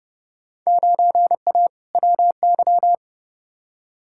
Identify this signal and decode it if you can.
Morse code, heard as a keyed tone, about 30 wpm, 700 Hz. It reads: 9A WY
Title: Field Day Exchanges Encore - 30wpm